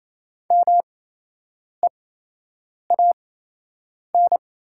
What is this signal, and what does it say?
Morse code, heard as a keyed tone, about 28 wpm, spaced as for 8 wpm, 700 Hz. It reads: MEAN
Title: 8x Character Spacing - Top 200 Words - 28wpm